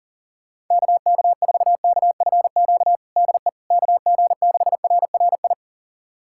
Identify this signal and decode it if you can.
Morse code, heard as a keyed tone, about 37 wpm, 700 Hz. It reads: KK4KFQ DE KG6RRI
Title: USA Callsign DE Callsign - 6 Characters - 40wpm